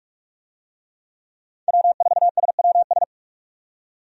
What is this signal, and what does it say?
Morse code, heard as a keyed tone, about 43 wpm, 700 Hz. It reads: W4SWS
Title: USA Call Signs - 5 Characters - 45wpm